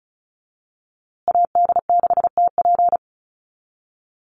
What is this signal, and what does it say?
Morse code, heard as a keyed tone, about 34 wpm, 700 Hz. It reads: AD6TP
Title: USA Call Signs Repeated 3x 35wpm